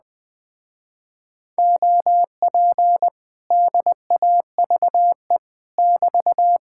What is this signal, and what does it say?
Morse code, heard as a keyed tone, about 20 wpm, 700 Hz. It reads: OP DAVE <BT>
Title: Speed-Racing Common QSO Phrases - 17wpm